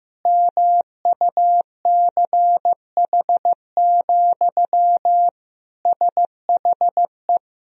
Morse code, heard as a keyed tone, about 16 wpm, 700 Hz. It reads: MUCH, SHE
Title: Sentences from Top 1000 Words - Review Part 3 - 15wpm